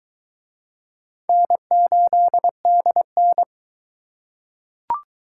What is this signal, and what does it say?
Morse code, heard as a keyed tone, about 23 wpm, 700 Hz. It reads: N8DN